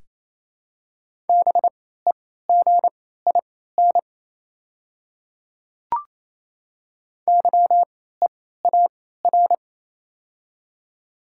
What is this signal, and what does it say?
Morse code, heard as a keyed tone, about 28 wpm, spaced as for 16 wpm, 700 Hz. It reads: BEGIN YEAR
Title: 3x Character Spacing - Top 300 Words - 28wpm